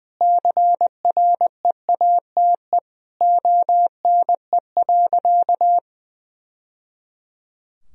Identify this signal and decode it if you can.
Morse code, heard as a keyed tone, about 20 wpm, 700 Hz. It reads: CREATE ONE.
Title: Sentences from Top 1000 Words - Review Part 3 - 20wpm